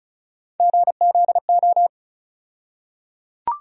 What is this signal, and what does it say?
Morse code, heard as a keyed tone, about 36 wpm, 700 Hz. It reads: GZO
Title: USA Call Sign Postfixes - 3 Letters 35wpm